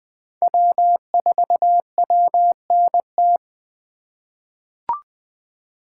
Morse code, heard as a keyed tone, about 20 wpm, 700 Hz. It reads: W4WNT
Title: K1USN SST QSO Practice - 20wpm